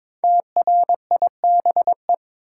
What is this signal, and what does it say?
Morse code, heard as a keyed tone, about 22 wpm, 690 Hz. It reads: TRIBE
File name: 5 Letter Words - First 3 Letters Given - 22wpm